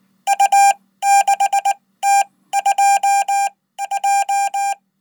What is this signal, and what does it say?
Morse code, heard as a keyed tone, about 20 wpm, spaced as for 14 wpm, 800 Hz. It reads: U6T2J
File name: Code